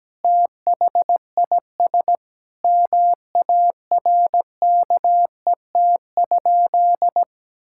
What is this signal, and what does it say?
Morse code, heard as a keyed tone, about 17 wpm, 700 Hz. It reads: THIS MARKET?
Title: Sentences from Top 1000 Words - Review Part 5 - 17wpm